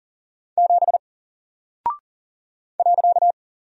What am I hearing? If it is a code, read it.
Morse code, heard as a keyed tone, about 39 wpm, 700 Hz. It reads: 7 .
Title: Single Character (Letter:Number:Punctuation:Prosign) 40wpm